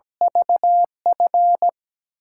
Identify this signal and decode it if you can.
Morse code, heard as a keyed tone, about 17 wpm, 700 Hz. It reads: VF
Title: USA Call Signs - 8x Character Spacing - 17wpm